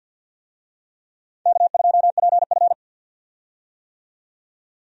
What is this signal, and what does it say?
Morse code, heard as a keyed tone, about 50 wpm, 700 Hz. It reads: K2PF
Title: USA Call Signs - 4 Characters - 50wpm